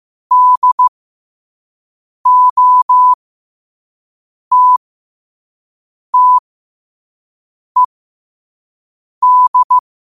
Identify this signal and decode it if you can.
Morse code, heard as a keyed tone, about 15 wpm, spaced as for 5 wpm, 1000 Hz. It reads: DOTTED
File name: 5wpm